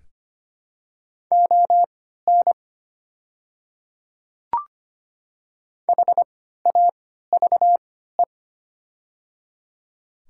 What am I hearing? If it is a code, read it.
Morse code, heard as a keyed tone, about 25 wpm, spaced as for 14 wpm, 700 Hz. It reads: ON HAVE
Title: 3x Character Spacing - Top 300 Words - 25wpm